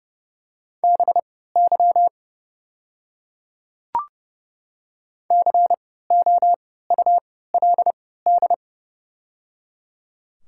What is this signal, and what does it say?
Morse code, heard as a keyed tone, about 30 wpm, spaced as for 17 wpm, 700 Hz. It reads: BY COULD
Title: 3x Character Spacing - Top 100 Words - 30wpm